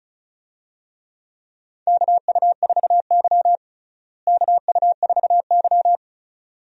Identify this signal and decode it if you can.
Morse code, heard as a keyed tone, about 36 wpm, 700 Hz. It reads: KU4Y KU4Y
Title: USA Call Signs - 4 Characters - Repeated 2x - 35wpm